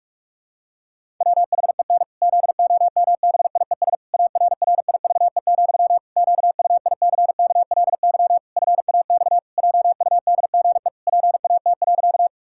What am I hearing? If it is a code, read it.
Morse code, heard as a keyed tone, about 41 wpm, 700 Hz. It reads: WHEN ZOMBIES ARRIVE, QUICKLY FAX JUDGE PAT.
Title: Pangram Challenge v2 - 45wpm